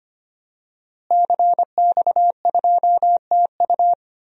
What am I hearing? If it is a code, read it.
Morse code, heard as a keyed tone, about 25 wpm, 700 Hz. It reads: CX2TU
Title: DXCC-100 Call Signs - 5 Characters - Repeated 2x - 25wpm